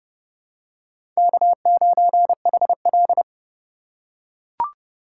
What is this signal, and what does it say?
Morse code, heard as a keyed tone, about 30 wpm, 700 Hz. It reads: K9HL
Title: USA Call Signs - 4 Characters - 2x Character Spacing - 30wpm